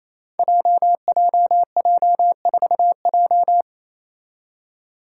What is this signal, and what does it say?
Morse code, heard as a keyed tone, about 28 wpm, 700 Hz. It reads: JJJ4J
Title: Five Challenging Letter-Number Combinations 28wpm